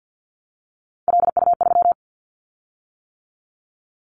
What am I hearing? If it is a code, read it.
Morse code, heard as a keyed tone, about 50 wpm, 690 Hz. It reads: LU3
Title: Callsign Prefixes for Top 100 DXCC - 50wpm